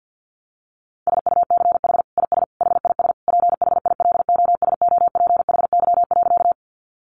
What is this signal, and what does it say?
Morse code, heard as a keyed tone, about 50 wpm, 700 Hz. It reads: SUCH IS HIS PHILOSOPHY.
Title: Sentences from Top 900 Words 50wpm